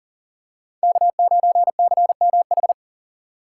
Morse code, heard as a keyed tone, about 40 wpm, 700 Hz. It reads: K9CMH
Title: USA Call Signs - 5 Characters - 4x Character Spacing - 40wpm